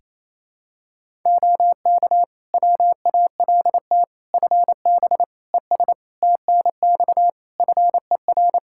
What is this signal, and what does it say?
Morse code, heard as a keyed tone, about 28 wpm, 700 Hz. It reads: OK WALT FB ES TNX FER
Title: Practice QSO - Exchange 2 - Info - 28wpm